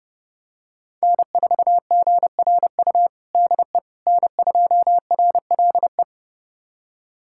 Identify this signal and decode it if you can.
Morse code, heard as a keyed tone, about 30 wpm, 700 Hz. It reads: N4GRU DE N2RLE
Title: USA Callsign DE Callsign - 5 Characters - 30wpm